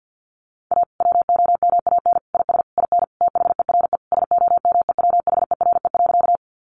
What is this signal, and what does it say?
Morse code, heard as a keyed tone, about 50 wpm, 700 Hz. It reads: A WOMAN IS IN THERE SOMEWHERE.
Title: Sentences from Top 1400 Words 50wpm